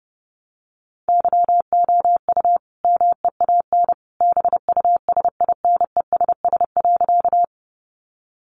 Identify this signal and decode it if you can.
Morse code, heard as a keyed tone, about 30 wpm, 700 Hz. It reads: YOU MEAN BUSINESS.
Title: Sentences from Top 600 Words 30wpm